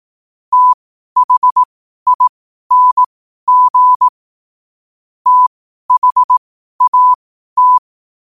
Morse code, heard as a keyed tone, about 18 wpm, spaced as for 12 wpm, 1000 Hz. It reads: THING THAT
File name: qotdfiles / 12wpm